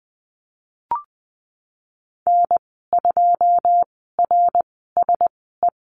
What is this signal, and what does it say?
Morse code, heard as a keyed tone, about 20 wpm, spaced as for 14 wpm, 700 Hz. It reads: N2RSE
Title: 128 Lesson 21g - Number 6 - Call Signs 20wpm